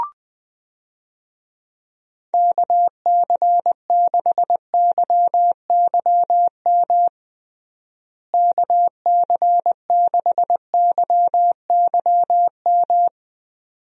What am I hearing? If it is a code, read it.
Morse code, heard as a keyed tone, about 20 wpm, 700 Hz. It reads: KC6YYM KC6YYM
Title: USA Call Signs - 6 Characters - Repeated 2x - 20wpm